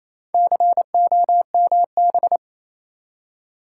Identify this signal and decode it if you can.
Morse code, heard as a keyed tone, about 27 wpm, 690 Hz. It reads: COMB
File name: Easy Words - Speed-Racing with Hints - 22wpm